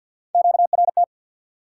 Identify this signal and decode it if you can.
Morse code, heard as a keyed tone, about 53 wpm, 700 Hz. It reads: QRT
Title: Top 180 QSO Elements 50wpm